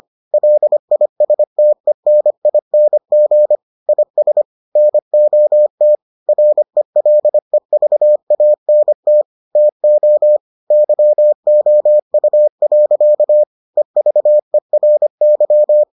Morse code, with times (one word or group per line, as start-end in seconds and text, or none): 0.3-3.6 LISTENING
3.9-4.4 IS
4.8-6.0 NOT
6.3-9.2 RELEVANT
9.6-10.4 TO
10.7-13.4 YOU.
13.8-15.8 EVERY